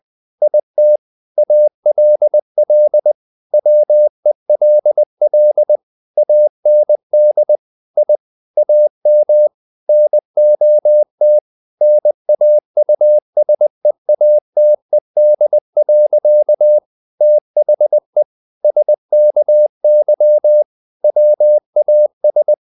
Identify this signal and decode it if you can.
Morse code, heard as a keyed tone, about 20 wpm, 600 Hz. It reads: IT ALL WELL AND I AM NOT NAUSEATED. THE SKY WAS
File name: the old man and the sea part 2 20wpm